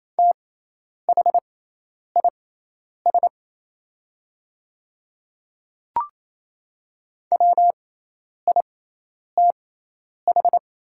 Morse code, heard as a keyed tone, about 28 wpm, spaced as for 10 wpm, 700 Hz. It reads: THIS WITH
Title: 6x Character Spacing - Top 100 Words - 28wpm